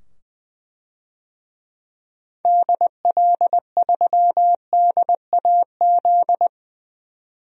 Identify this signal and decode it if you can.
Morse code, heard as a keyed tone, about 20 wpm, 700 Hz. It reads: DL3DAZ
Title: DXCC-100 Call Signs - 6 Characters - 20wpm